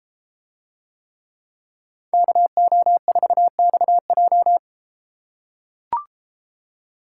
Morse code, heard as a keyed tone, about 31 wpm, 700 Hz. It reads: KO4XJ